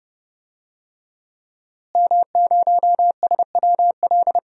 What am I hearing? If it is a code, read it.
Morse code, heard as a keyed tone, about 30 wpm, 700 Hz. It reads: M0SWL
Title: DXCC-100 Call Signs - 5 Characters - Repeated 3x - 30wpm